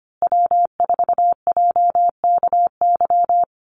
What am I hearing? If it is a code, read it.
Morse code, heard as a keyed tone, about 25 wpm, 700 Hz. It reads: W4JKY
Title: USA Call Signs Repeated 2x 25wpm